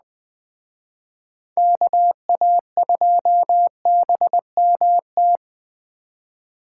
Morse code, heard as a keyed tone, about 20 wpm, 700 Hz. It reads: KA2BMT